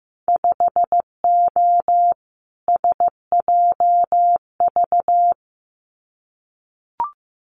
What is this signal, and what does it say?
Morse code, heard as a keyed tone, about 15 wpm, 700 Hz. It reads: 5O SJV
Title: Winter Field Day Exchanges Encore 15wpm